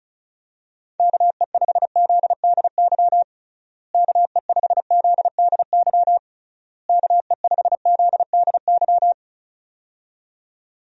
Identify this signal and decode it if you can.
Morse code, heard as a keyed tone, about 34 wpm, 700 Hz. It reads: KE5ZDY KE5ZDY KE5ZDY
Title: USA Call Signs - 6 Characters - Repeated 3x - 35wpm